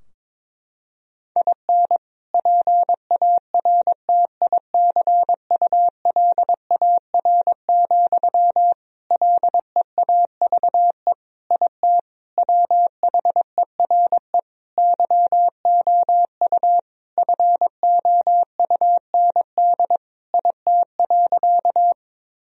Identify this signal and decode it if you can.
Morse code, heard as a keyed tone, about 22 wpm, 700 Hz. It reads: IN PARTICULAR, LEAVE IT WHERE YOU FOUND IT.
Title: Sentences from Top 1000 Words - Review Part 4 - 22wpm